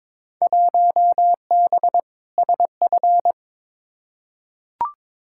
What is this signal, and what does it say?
Morse code, heard as a keyed tone, about 22 wpm, 700 Hz. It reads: 1B SF
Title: Field Day Exchanges Encore - 22wpm